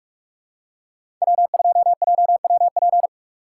Morse code, heard as a keyed tone, about 47 wpm, 700 Hz. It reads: W2JWP
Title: USA Call Signs - 5 Characters - 3x Character Spacing - 45wpm